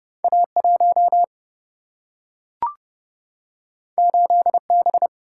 Morse code, heard as a keyed tone, about 31 wpm, 700 Hz. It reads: A1 8B
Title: Single-Letter Single-Number 30wpm